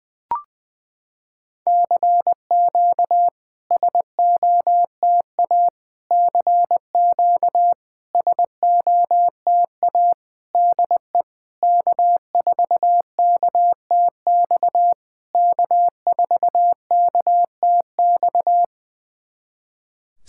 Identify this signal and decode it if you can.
Morse code, heard as a keyed tone, about 20 wpm, 700 Hz. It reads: CQ SOTA CQ SOTA DE K4KTX K4KTX
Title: SOTA QSO Practice 20wpm